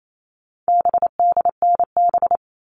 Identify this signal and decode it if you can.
Morse code, heard as a keyed tone, about 28 wpm, 700 Hz. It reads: BDNB
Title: All Cut Numbers - 4-Digit 28wpm